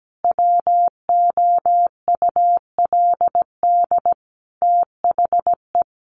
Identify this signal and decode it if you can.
Morse code, heard as a keyed tone, about 17 wpm, 700 Hz. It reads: WOULD THE
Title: Sentences from Top 1500 Words 17wpm